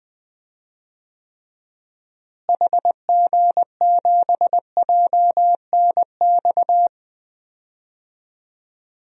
Morse code, heard as a keyed tone, about 21 wpm, 700 Hz. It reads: HG7JNX
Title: DXCC-100 Call Signs - 6 Characters - Repeated 3x - 20wpm